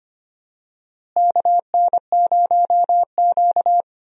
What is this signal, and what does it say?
Morse code, heard as a keyed tone, about 25 wpm, 700 Hz. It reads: KN0Q
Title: USA Call Signs - 4 Characters - Repeated 3x - 25wpm